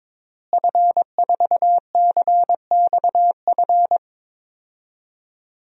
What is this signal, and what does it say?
Morse code, heard as a keyed tone, about 22 wpm, 700 Hz. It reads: F4CXF
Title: DXCC-100 Call Signs - 5 Characters - Repeated 3x - 22wpm